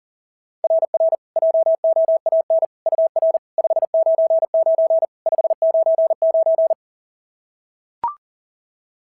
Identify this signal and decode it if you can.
Morse code, heard as a keyed tone, about 40 wpm, 690 Hz. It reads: RR JOAN UR 599 599
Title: SOTA QSO Practice 40wpm